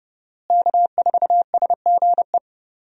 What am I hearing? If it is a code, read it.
Morse code, heard as a keyed tone, about 30 wpm, 700 Hz. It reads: K4SGE
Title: USA Call Signs - 5 Characters - Repeated 3x - 30wpm